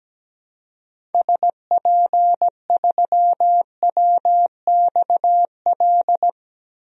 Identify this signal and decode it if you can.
Morse code, heard as a keyed tone, about 17 wpm, 700 Hz. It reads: SP3WXL
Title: DXCC-100 Call Signs - 6 Characters - 17wpm